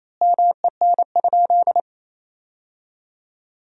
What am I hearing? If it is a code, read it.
Morse code, heard as a keyed tone, about 28 wpm, 700 Hz. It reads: MEN?
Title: 4x Word-Spacing - Sentences from Top 200 Words - 28wpm